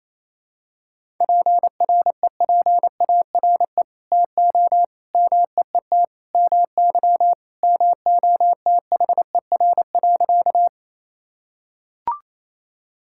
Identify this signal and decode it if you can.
Morse code, heard as a keyed tone, about 28 wpm, 700 Hz. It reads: PREPARE TO MEET MY MOTHER.
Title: Sentences from Top 1000 Words - Review Part 6 - 28wpm